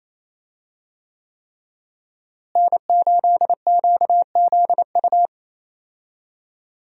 Morse code, heard as a keyed tone, about 28 wpm, 700 Hz. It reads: N8QZU
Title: USA Call Signs - 5 Characters - Repeated 3x - 28wpm